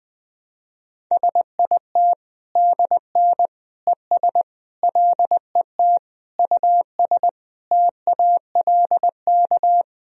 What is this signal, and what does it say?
Morse code, heard as a keyed tone, about 20 wpm, 700 Hz. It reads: SIT DN ES LET US TALK
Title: Sentences from Top 500 Words and Abbreviations 20wpm